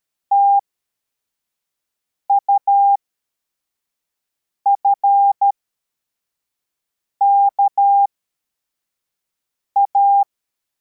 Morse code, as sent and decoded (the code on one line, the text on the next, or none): - ..- ..-. -.- .-
TUFKA